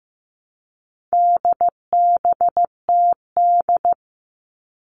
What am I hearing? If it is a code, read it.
Morse code, heard as a keyed tone, about 15 wpm, 700 Hz. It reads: DBTD
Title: All Cut Numbers - 4-Digit 15wpm